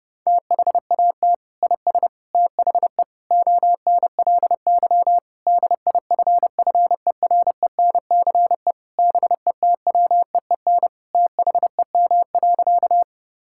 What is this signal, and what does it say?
Morse code, heard as a keyed tone, about 30 wpm, 700 Hz. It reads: THAT IS THE ONLY DIFFERENCE BETWEEN THEM.
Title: Sentences from Top 1000 Words - Review Part 5 - 30wpm